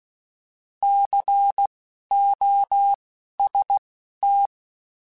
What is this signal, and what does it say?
Morse code, heard as a keyed tone, about 16 wpm, spaced as for 12 wpm, 800 Hz. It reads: COST
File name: Alphabetical / Alphabetical mp3s / Cs @10